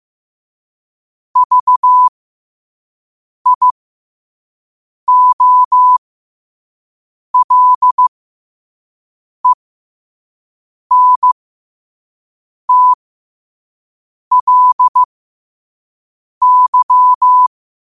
Morse code, heard as a keyed tone, about 15 wpm, spaced as for 5 wpm, 1000 Hz. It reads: VIOLENTLY
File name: qotdfiles / 5wpm